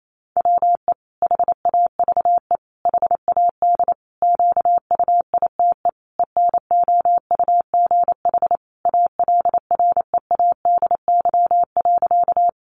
Morse code, heard as a keyed tone, about 28 wpm, 700 Hz. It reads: WE HAVE HAD QUITE ENOUGH ALREADY.
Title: Sentences from Top 700 Words 28wpm